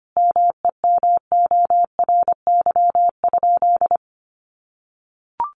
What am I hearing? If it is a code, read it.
Morse code, heard as a keyed tone, about 25 wpm, 700 Hz. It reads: MEMORY?
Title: Sentences from Top 1300 Words 25wpm